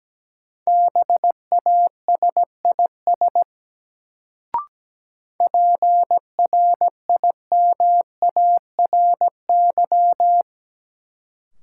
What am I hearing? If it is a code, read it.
Morse code, heard as a keyed tone, about 17 wpm, 700 Hz. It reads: BASIS PRIMARY
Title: Sentences from Top 1000 Words - Review Part 5 - New Word Drill - 17wpm